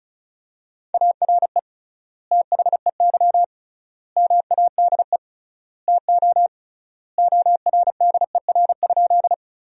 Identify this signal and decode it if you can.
Morse code, heard as a keyed tone, about 34 wpm, 700 Hz. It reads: ARE THEY MADE TO ORDER?
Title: 3x Word-Spacing - Sentences from Top 400 Words - 35wpm